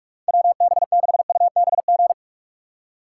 Morse code, heard as a keyed tone, about 44 wpm, 700 Hz. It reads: WB6UBG
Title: USA Call Signs - 6x Character Spacing - 45wpm